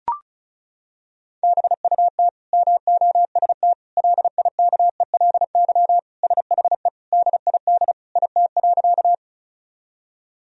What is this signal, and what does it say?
Morse code, heard as a keyed tone, about 36 wpm, 700 Hz. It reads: BUT MOST LIKELY SHE DID IT.
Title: Sentences from Top 1000 Words - Review Part 4 - 35wpm